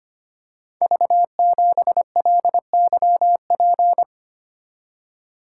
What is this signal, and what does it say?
Morse code, heard as a keyed tone, about 25 wpm, 700 Hz. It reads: V7LYP
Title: Code Groups - 5 Letter:Numbers - 25wpm